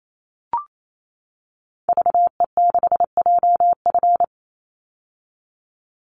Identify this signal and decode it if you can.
Morse code, heard as a keyed tone, about 28 wpm, 700 Hz. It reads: VE6JF
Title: Call Signs - 28wpm